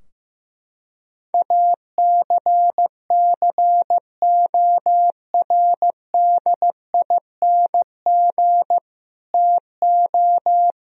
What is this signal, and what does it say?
Morse code, heard as a keyed tone, about 15 wpm, 700 Hz. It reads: ACCORDING TO
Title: Sentences from Top 1000 Words - Review Part 4 - 15wpm